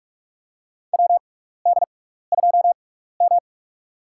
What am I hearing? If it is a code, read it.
Morse code, heard as a keyed tone, about 46 wpm, spaced as for 16 wpm, 700 Hz. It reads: WD2M
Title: USA Call Signs - 4 Characters - 6x Character Spacing - 45wpm